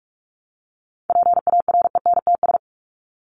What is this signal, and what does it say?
Morse code, heard as a keyed tone, about 44 wpm, 700 Hz. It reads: PARENTS
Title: ES:S Suffix - Top 100 Words with Base Words 45wpm